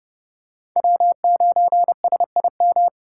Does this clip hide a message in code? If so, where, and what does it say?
Yes: Morse code from 0.8 to 2.9 s, heard as a keyed tone, about 30 wpm, 700 Hz. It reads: W9SIM